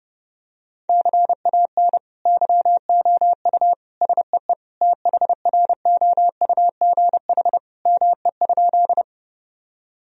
Morse code, heard as a keyed tone, about 30 wpm, 700 Hz. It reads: CAN YOU SEE THROUGH ME?
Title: Sentences from Top 1000 Words - Review Part 1 - 30wpm